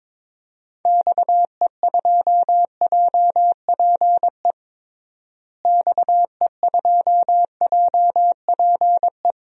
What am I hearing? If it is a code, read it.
Morse code, heard as a keyed tone, about 22 wpm, 700 Hz. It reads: XE2JPE XE2JPE
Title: DXCC-100 Call Signs - 6 Characters - Repeated 2x - 22wpm